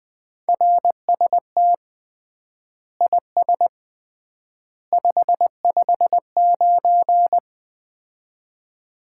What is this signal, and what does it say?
Morse code, heard as a keyed tone, about 20 wpm, 700 Hz. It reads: RST IS 559
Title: Common QSO Phrases - 3x Word-Spacing - 20wpm